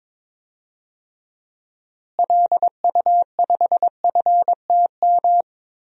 Morse code, heard as a keyed tone, about 22 wpm, 700 Hz. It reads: LU5FTM
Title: DXCC-100 Call Signs - 6 Characters - Repeated 2x - 22wpm